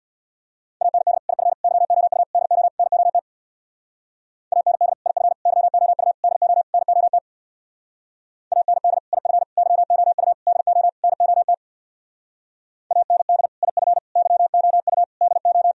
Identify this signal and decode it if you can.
Morse code, heard as a keyed tone, about 64 wpm, 700 Hz. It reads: AND IF YOU DO NOT AND IF YOU DO NOT AND IF YOU DO NOT AND IF YOU DO